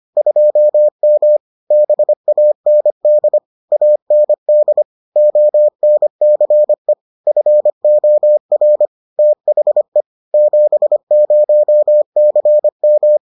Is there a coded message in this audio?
Yes: Morse code from 0.2 to 13.2 s, heard as a keyed tone, about 25 wpm, 600 Hz. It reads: 2M BAND AND ONCE FOR THE 70CM